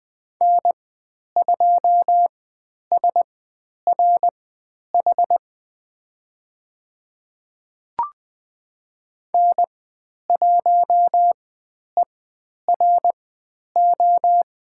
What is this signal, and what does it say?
Morse code, heard as a keyed tone, about 20 wpm, spaced as for 10 wpm, 700 Hz. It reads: N2SRH N1ERO
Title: 095 Lesson 16g - Letter c - Call Signs 20wpm